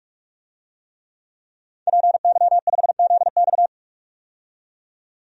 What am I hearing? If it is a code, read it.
Morse code, heard as a keyed tone, about 47 wpm, 700 Hz. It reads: PY5ZX
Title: DXCC-100 Call Signs - 5 Characters - 45wpm